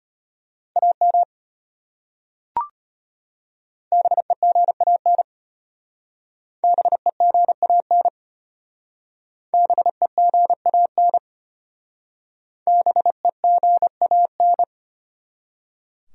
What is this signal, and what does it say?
Morse code, heard as a keyed tone, about 32 wpm, 700 Hz. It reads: AM BEGAN BEGAN BEGAN BEGAN